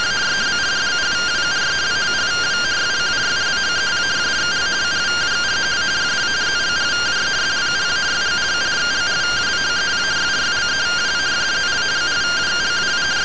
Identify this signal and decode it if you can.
Two-tone teleprinter signal, about 170 Hz shift.